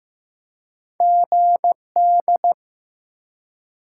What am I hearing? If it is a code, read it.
Morse code, heard as a keyed tone, about 15 wpm, 700 Hz. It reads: GD
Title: Top 90 QSO Elements 15wpm